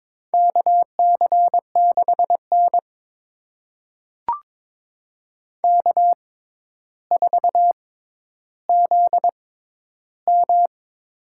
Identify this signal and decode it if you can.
Morse code, heard as a keyed tone, about 22 wpm, spaced as for 11 wpm, 700 Hz. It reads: KC6N K4ZM
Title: USA Call Signs - 4 Characters - 6x Character Spacing - 22wpm